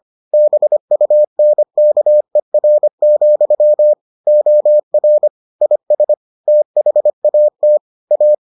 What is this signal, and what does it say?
Morse code, heard as a keyed tone, about 25 wpm, 600 Hz. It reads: BUNKER, OR IS THAT A